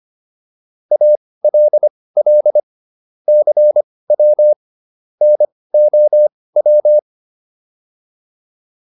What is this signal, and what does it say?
Morse code, heard as a keyed tone, about 25 wpm, spaced as for 18 wpm, 600 Hz. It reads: ALL CW NOW